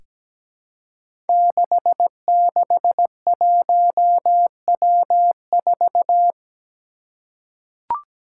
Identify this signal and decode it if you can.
Morse code, heard as a keyed tone, about 17 wpm, 700 Hz. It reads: 661W4
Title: Five Challenging Letter-Number Combinations 17wpm